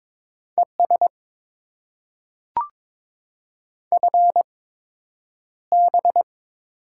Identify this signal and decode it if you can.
Morse code, heard as a keyed tone, about 22 wpm, spaced as for 9 wpm, 700 Hz. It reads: ES FB
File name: Top 20 QSO Elements - 8x Character Spacing - 22wpm